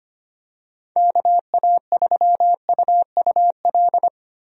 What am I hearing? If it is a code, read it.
Morse code, heard as a keyed tone, about 25 wpm, 700 Hz. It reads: KA3UUL
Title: USA Call Signs - 6 Characters - 25wpm